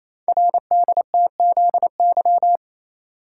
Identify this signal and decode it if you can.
Morse code, heard as a keyed tone, about 27 wpm, 700 Hz. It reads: RDTZY
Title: Code Groups - 5 Letters - 28wpm